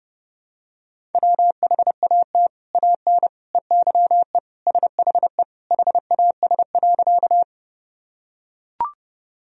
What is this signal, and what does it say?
Morse code, heard as a keyed tone, about 32 wpm, 700 Hz. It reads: WHAT AN EYE SHE HAS.
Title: Sentences from Top 1000 Words - Review Part 1 - 30wpm